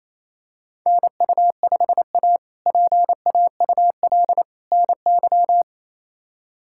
Morse code, heard as a keyed tone, about 28 wpm, 700 Hz. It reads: NU5A PAUL NY
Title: K1USN SST QSO Practice - 28wpm